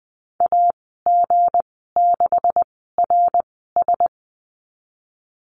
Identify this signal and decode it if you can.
Morse code, heard as a keyed tone, about 20 wpm, spaced as for 15 wpm, 700 Hz. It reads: AG6RS
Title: 167 Lesson 27g - Number 7 - Call Signs 20wpm